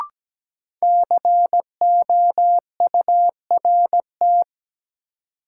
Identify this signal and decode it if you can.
Morse code, heard as a keyed tone, about 17 wpm, 700 Hz. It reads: COURT
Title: Sentences from Top 1600 Words - New Word Drill Alt - 17wpm